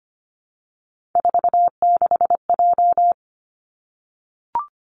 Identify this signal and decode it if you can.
Morse code, heard as a keyed tone, about 25 wpm, 700 Hz. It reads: 46J
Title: Three Challenging Letter-Number Combinations 25wpm